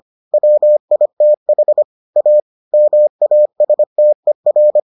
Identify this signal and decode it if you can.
Morse code, heard as a keyed tone, about 25 wpm, 600 Hz. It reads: WITH A MASTER